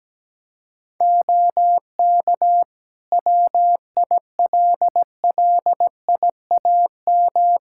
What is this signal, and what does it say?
Morse code, heard as a keyed tone, about 17 wpm, 700 Hz. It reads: OK WILLIAM
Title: Practice QSO - Exchange 4 - Ending - 17wpm